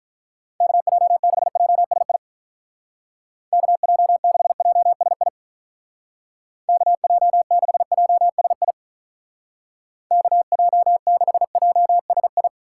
Morse code, heard as a keyed tone, about 44 wpm, 700 Hz. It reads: KJ6JSI KJ6JSI KJ6JSI KJ6JSI